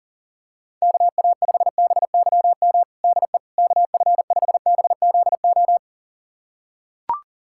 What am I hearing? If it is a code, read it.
Morse code, heard as a keyed tone, about 40 wpm, 700 Hz. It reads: KA5BYM DE KF5BZO
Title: USA Callsign DE Callsign 40wpm